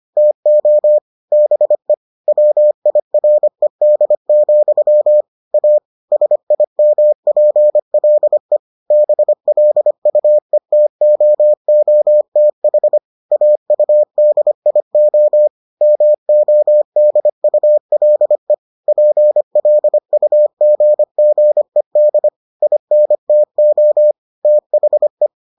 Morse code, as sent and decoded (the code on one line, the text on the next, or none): - --- / -... / .-- .. .-. -.. --..-- / .- / ... .. -- .--. .-.. / -... .-.. ..- . - --- --- - .... / .- ..- -.. .. --- / -- --- -.. ..- .-.. / .--. .-.. ..- --. --. -.. / .. -. - --- / - .... .
TO BE WIRED, A SIMPLE BLUETOOTH AUDIO MODULE PLUGGED INTO THE